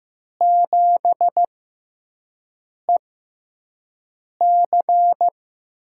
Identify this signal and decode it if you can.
Morse code, heard as a keyed tone, about 15 wpm, spaced as for 5 wpm, 700 Hz. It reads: 7EC
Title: USA Call Signs - 6x Character Spacing - 15wpm